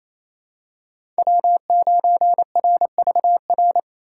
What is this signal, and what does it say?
Morse code, heard as a keyed tone, about 28 wpm, 700 Hz. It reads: W9RVR
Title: USA Call Signs - 5 Characters - 4x Character Spacing - 28wpm